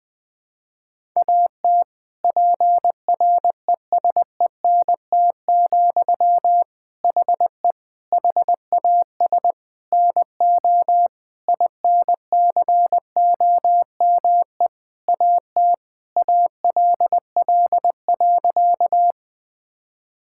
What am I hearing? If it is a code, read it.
Morse code, heard as a keyed tone, about 20 wpm, 700 Hz. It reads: AT PRESENT, HE HAS NO INCOME AT ALL.
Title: Sentences from Top 1000 Words - Review Part 5 - 20wpm